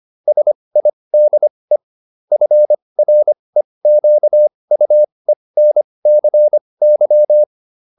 Morse code, heard as a keyed tone, about 25 wpm, spaced as for 20 wpm, 600 Hz. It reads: SIDE FREQUENCY